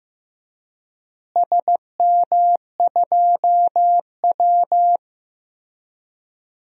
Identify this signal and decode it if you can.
Morse code, heard as a keyed tone, about 15 wpm, 700 Hz. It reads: SM2W
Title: DXCC-100 Call Signs - 4 Characters - Repeated 3x - 15wpm